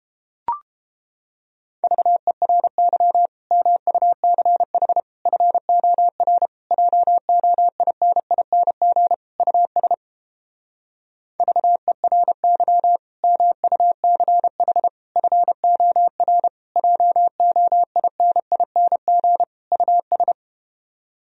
Morse code, heard as a keyed tone, about 31 wpm, 700 Hz. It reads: VERY MUCH FOR JOINING US VERY MUCH FOR JOINING US